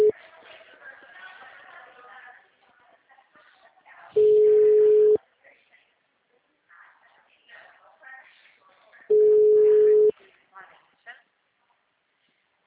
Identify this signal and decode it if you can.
Ringback tone (Europe, 425 Hz), with speech.
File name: out- -2003- - - .1054